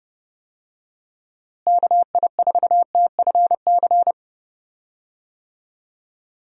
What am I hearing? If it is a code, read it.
Morse code, heard as a keyed tone, about 31 wpm, 700 Hz. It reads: KI4TFC
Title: USA Call Signs - 6 Characters - Repeated 3x - 30wpm